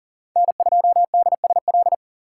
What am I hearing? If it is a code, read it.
Morse code, heard as a keyed tone, about 40 wpm, 700 Hz. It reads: N2DSL
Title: USA Call Signs - 5 Characters - 2x Character Spacing - 40wpm